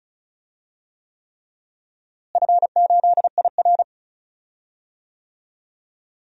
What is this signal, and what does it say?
Morse code, heard as a keyed tone, about 34 wpm, 700 Hz. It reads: F8IR
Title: DXCC-100 Call Signs - 4 Characters - Repeated 3x - 35wpm